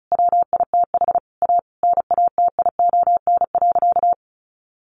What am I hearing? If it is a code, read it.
Morse code, heard as a keyed tone, about 34 wpm, 700 Hz. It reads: WITH A NATION.
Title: Sentences from Top 1200 Words 35wpm